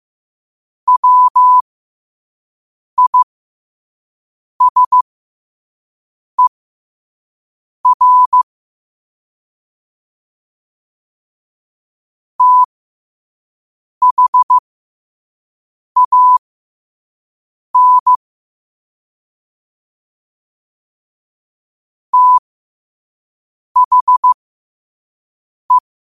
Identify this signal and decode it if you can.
Morse code, heard as a keyed tone, about 15 wpm, spaced as for 5 wpm, 1000 Hz. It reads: WISER THAN THE
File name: qotdfiles / 5wpm